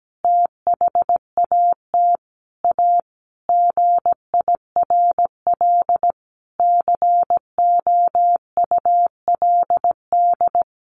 Morse code, heard as a keyed tone, about 17 wpm, 700 Hz. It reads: THAT A GIRL COULD
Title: Sentences from Top 1400 Words 17wpm